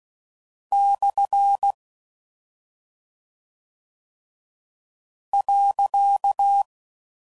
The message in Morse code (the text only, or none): / .